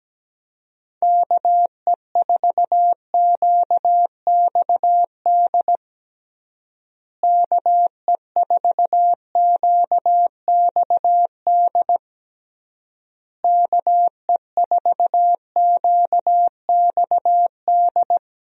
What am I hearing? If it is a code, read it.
Morse code, heard as a keyed tone, about 17 wpm, 700 Hz. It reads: KE4QXD KE4QXD KE4QXD
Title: USA Call Signs - 6 Characters - Repeated 3x - 17wpm